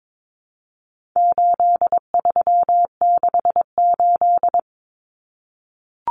Morse code, heard as a keyed tone, about 22 wpm, 700 Hz. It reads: 8368
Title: Numbers - 4-Digit 22wpm